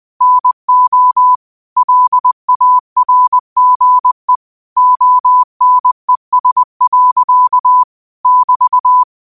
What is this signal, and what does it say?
Morse code, heard as a keyed tone, about 20 wpm, 1000 Hz. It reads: NO LARGE ONES. <BT>
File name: 20wpm